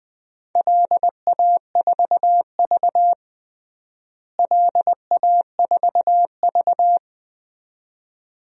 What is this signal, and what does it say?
Morse code, heard as a keyed tone, about 20 wpm, 700 Hz. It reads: LA4V LA4V
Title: DXCC-100 Call Signs - 4 Characters - Repeated 2x - 20wpm